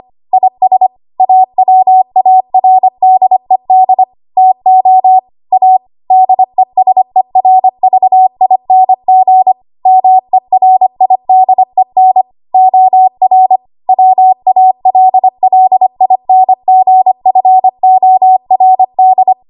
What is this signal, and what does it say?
Morse code, heard as a keyed tone, about 25 wpm, 750 Hz. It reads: IS AWARDED TO A DESERVING MERIDEN OR WALLINGFORD
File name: Morse / 25WPM